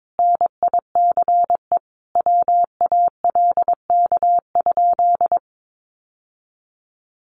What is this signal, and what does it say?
Morse code, heard as a keyed tone, about 22 wpm, 700 Hz. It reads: NICE WALK?
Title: Sentences from Top 1100 Words 22wpm